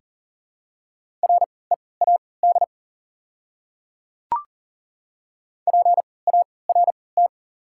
Morse code, heard as a keyed tone, about 37 wpm, spaced as for 22 wpm, 690 Hz. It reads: READ PART
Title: 3x Character Spacing - Top 200 Words - 40wpm